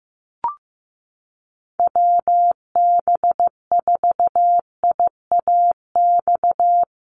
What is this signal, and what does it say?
Morse code, heard as a keyed tone, about 15 wpm, 700 Hz. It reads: WB4IAX
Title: USA Call Signs Repeated 3x 15wpm